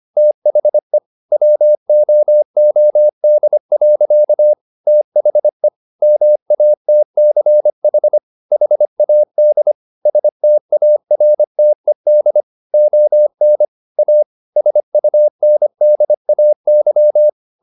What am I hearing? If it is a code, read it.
Morse code, heard as a keyed tone, about 25 wpm, 600 Hz. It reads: THE WOOD. THE MATCH HAD STARTED ON A SUNDAY